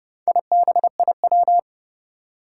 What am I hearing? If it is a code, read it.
Morse code, heard as a keyed tone, about 30 wpm, 700 Hz. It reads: IBIW
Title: Four Challenging Letter-Number Combinations 30wpm